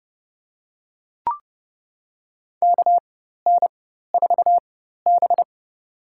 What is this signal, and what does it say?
Morse code, heard as a keyed tone, about 30 wpm, spaced as for 14 wpm, 700 Hz. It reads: KN4B
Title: USA Call Signs - 4 Characters - 4x Character Spacing - 30wpm